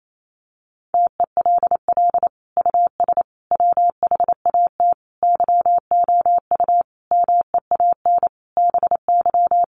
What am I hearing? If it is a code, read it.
Morse code, heard as a keyed tone, about 28 wpm, 700 Hz. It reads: TELL US WHAT YOU MEAN BY
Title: Sentences from Top 1000 Words 28wpm